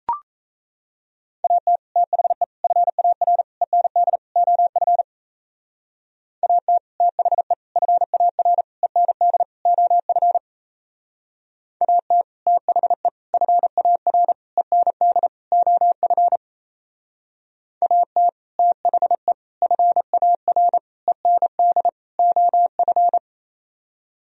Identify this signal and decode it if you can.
Morse code, heard as a keyed tone, about 35 wpm, 700 Hz. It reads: AT THE FAR END OF AT THE FAR END OF AT THE FAR END OF AT THE FAR END OF
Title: Speed-Racing - Sets of 5 Words 28wpm